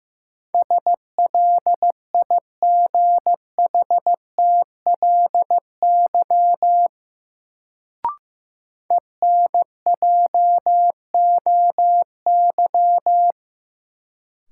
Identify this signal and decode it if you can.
Morse code, heard as a keyed tone, about 15 wpm, 700 Hz. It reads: SLIGHTLY ENJOY
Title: Sentences from Top 1300 Words - New Word Drill Alt - 15wpm